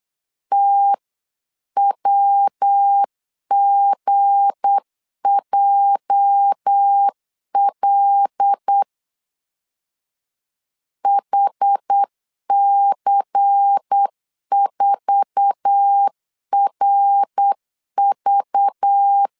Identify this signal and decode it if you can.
Morse code, heard as a keyed tone, about 8 wpm, 800 Hz. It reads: TWGJL HC4RV
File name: CW / DET / 1 9 1500C S A 8wpm 5637